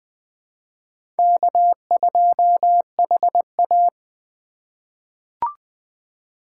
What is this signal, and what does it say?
Morse code, heard as a keyed tone, about 20 wpm, 700 Hz. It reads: K2HA
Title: 2023 POTA Hunter Call Signs - 20wpm